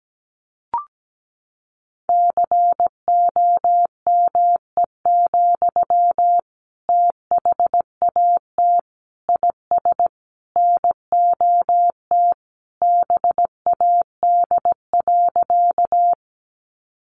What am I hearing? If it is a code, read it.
Morse code, heard as a keyed tone, about 17 wpm, 700 Hz. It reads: COME, THAT IS NOT BAD.
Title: Sentences from Top 700 Words 17wpm